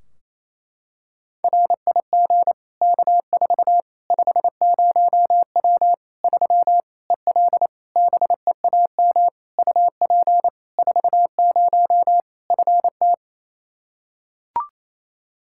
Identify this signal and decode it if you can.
Morse code, heard as a keyed tone, about 28 wpm, 700 Hz. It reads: RIG K4 50W 3 EL BEAM UP 40 FT
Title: Practice QSO - Exchange 2 - Info - 28wpm